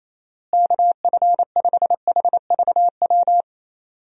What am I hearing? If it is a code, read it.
Morse code, heard as a keyed tone, about 28 wpm, 700 Hz. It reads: KF5HVW
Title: USA Call Signs - 6 Characters - Repeated 3x - 28wpm